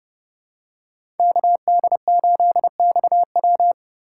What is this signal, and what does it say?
Morse code, heard as a keyed tone, about 29 wpm, 700 Hz. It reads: KD8XW
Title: USA Call Signs - 5 Characters - 2x Character Spacing - 30wpm